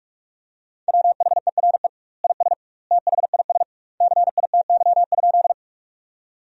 Morse code, heard as a keyed tone, about 46 wpm, 700 Hz. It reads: WHERE IS THIS CITY?
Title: 2x Word-Spacing - Sentences from Top 300 Words - 45wpm